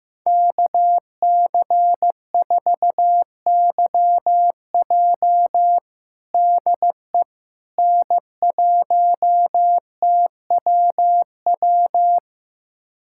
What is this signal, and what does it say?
Morse code, heard as a keyed tone, about 15 wpm, 700 Hz. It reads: KC4YJ DE N1TWW
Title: USA Callsign DE Callsign - 5 Characters - 15wpm